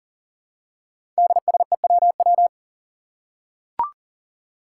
Morse code, heard as a keyed tone, about 41 wpm, 700 Hz. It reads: DSEWW
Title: Five Challenging Letter-Number Combinations 40wpm